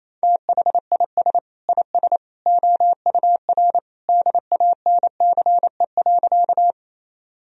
Morse code, heard as a keyed tone, about 28 wpm, 700 Hz. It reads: THIS IS OUR DANCE.
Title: Sentences from Top 1000 Words - Review Part 4 - 28wpm